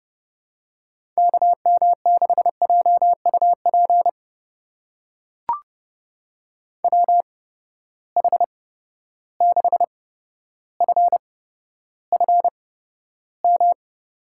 Morse code, heard as a keyed tone, about 29 wpm, spaced as for 13 wpm, 700 Hz. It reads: KM6JUP WH6FFM
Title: USA Call Signs - 6 Characters - 8x Character Spacing - 30wpm